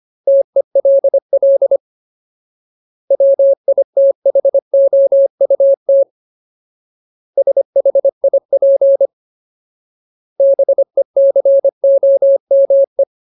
Morse code, heard as a keyed tone, about 25 wpm, 550 Hz. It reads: TELL WITHOUT SHIP BECOME
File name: Ultimate Word List / 25WPM Ultimate